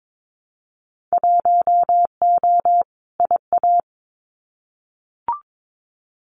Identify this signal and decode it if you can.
Morse code, heard as a keyed tone, about 22 wpm, 700 Hz. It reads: 1O IA
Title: Winter Field Day Exchanges Encore 22wpm